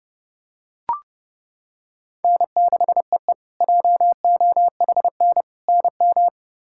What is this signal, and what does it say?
Morse code, heard as a keyed tone, about 30 wpm, 700 Hz. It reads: N6EE JOHN NM
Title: K1USN SST QSO Practice - 30wpm